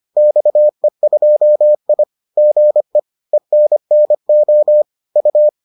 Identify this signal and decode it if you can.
Morse code, heard as a keyed tone, about 25 wpm, 600 Hz. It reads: XE2I GE ENNO U